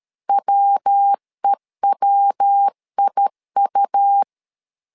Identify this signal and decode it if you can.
Morse code, heard as a keyed tone, about 13 wpm, 800 Hz. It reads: WEWIU
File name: LEZIONI / 3 MWRO / 3 MW 12wpm 1633